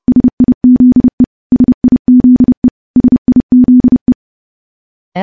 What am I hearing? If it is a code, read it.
Morse code, heard as a keyed tone, about 30 wpm, 250 Hz. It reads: SIZE SIZE SIZE
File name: morse-training-30WPM